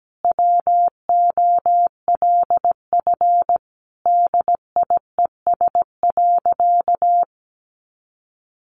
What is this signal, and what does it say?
Morse code, heard as a keyed tone, about 17 wpm, 700 Hz. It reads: WOLF DIES.
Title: Additional Easy Phrases 17wpm